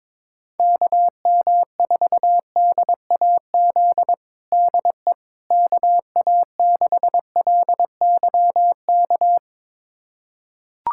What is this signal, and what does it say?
Morse code, heard as a keyed tone, about 21 wpm, 700 Hz. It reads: KM4DAZ DE KA6LYK
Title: USA Callsign DE Callsign 22wpm